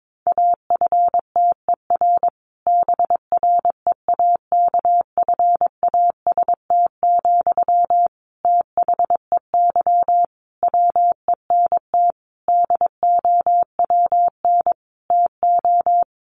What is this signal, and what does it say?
Morse code, heard as a keyed tone, about 22 wpm, 700 Hz. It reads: AFTER BREAKFAST, THEY WENT DOWN TO
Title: Sentences from Top 1500 Words 22wpm